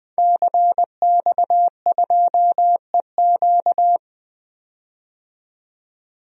Morse code, heard as a keyed tone, about 20 wpm, 700 Hz. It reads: CX2EQ
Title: DXCC-100 Call Signs - 5 Characters - 20wpm